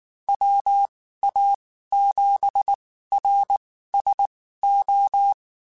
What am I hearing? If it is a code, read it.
Morse code, heard as a keyed tone, about 19 wpm, spaced as for 14 wpm, 800 Hz. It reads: WA7RSO
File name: QSOs / QSOs-mp3s / WD0FBT 1@18